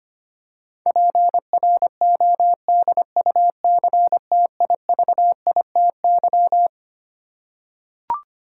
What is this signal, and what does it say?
Morse code, heard as a keyed tone, about 25 wpm, 700 Hz. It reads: PRODUCTIVITY
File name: Twelve Letter Words - 25wpm